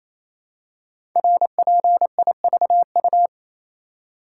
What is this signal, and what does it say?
Morse code, heard as a keyed tone, about 27 wpm, 700 Hz. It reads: RPIVU
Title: Code Groups - 5 Letters - 28wpm